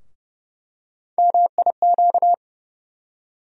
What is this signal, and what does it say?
Morse code, heard as a keyed tone, about 30 wpm, 700 Hz. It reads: MIQ
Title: USA Call Sign Postfixes - 3 Letters 30wpm